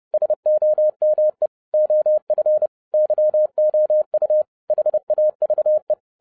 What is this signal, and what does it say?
Morse code, heard as a keyed tone, about 30 wpm, 610 Hz. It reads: SOME OF YOU HAVE
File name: 2011-07-mp3 / n40 EXCITING QRP CW OPERATION 30wpm